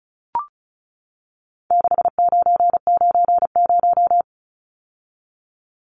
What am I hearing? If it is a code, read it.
Morse code, heard as a keyed tone, about 34 wpm, 700 Hz. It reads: 6990
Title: Numbers - 4-Digit 35wpm